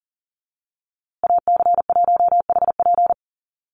Morse code, heard as a keyed tone, about 40 wpm, 700 Hz. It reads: AC1HP
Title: USA Call Signs Repeated 2x 40wpm